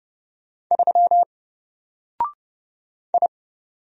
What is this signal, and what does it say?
Morse code, heard as a keyed tone, about 30 wpm, 700 Hz. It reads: 3 I
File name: Single-Letter-Number-30wpm